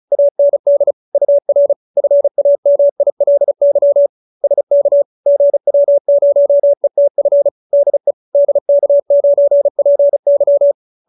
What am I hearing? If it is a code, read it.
Morse code, heard as a keyed tone, about 34 wpm, 570 Hz. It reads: AND UR FAMILY SK GW0ETF DE DK9PY